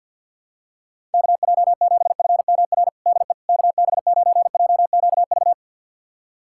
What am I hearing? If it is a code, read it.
Morse code, heard as a keyed tone, about 48 wpm, 700 Hz. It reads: KJ7FMR DE KB9JQV
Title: USA Callsign DE Callsign 50wpm